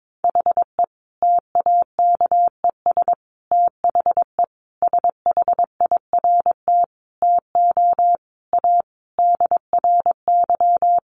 Morse code, heard as a keyed tone, about 22 wpm, 700 Hz. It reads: HE TAKES THE SHIRT TO A DRY
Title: Additional Easy Phrases 22wpm